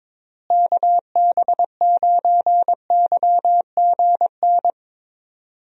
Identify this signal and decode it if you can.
Morse code, heard as a keyed tone, about 21 wpm, 700 Hz. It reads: KB9YGN
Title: USA Call Signs - 6 Characters - Repeated 3x - 22wpm